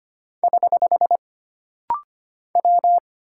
Morse code, heard as a keyed tone, about 25 wpm, 700 Hz. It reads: <HH> W
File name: Single Character (Letter:Number:Punctuation:Prosign) - Rapid-Fire - 25wpm